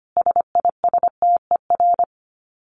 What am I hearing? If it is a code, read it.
Morse code, heard as a keyed tone, about 25 wpm, 700 Hz. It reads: SISTER
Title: Dit Challenge 25wpm